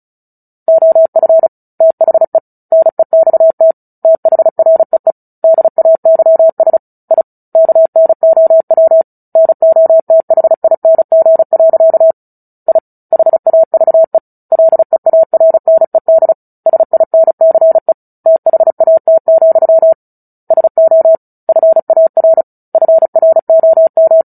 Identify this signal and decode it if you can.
Morse code, heard as a keyed tone, about 34 wpm, 650 Hz. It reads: OF THE NEXT THREE DAYS I KNOW NOTHING. I HAVE LEARNED SINCE THAT, SO FAR FROM
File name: Book Two / 09-Chapter 9 - 35 WPM